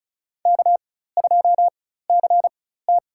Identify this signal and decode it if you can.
Morse code, heard as a keyed tone, about 36 wpm, spaced as for 16 wpm, 700 Hz. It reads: K2CT
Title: USA Call Signs - 4x Character Spacing - 35wpm